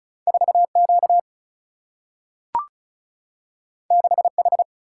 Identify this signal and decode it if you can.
Morse code, heard as a keyed tone, about 36 wpm, 700 Hz. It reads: 4 Q 6 H
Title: Single-Letter Single-Number 35wpm